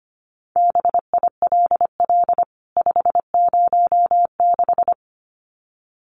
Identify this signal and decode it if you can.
Morse code, heard as a keyed tone, about 25 wpm, 700 Hz. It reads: BILL 506
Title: CWT Exchange 25wpm